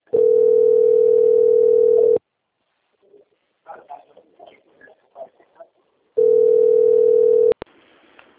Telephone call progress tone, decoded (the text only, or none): ringback tone